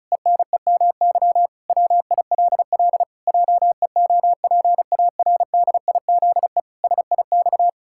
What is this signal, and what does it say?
Morse code, heard as a keyed tone, about 36 wpm, 700 Hz. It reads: ENEMY WILL JEOPARDIZE SIX
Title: Pangram Challenge v2 - 35wpm